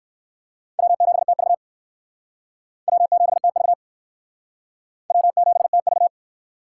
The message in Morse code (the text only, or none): W7TV W7TV W7TV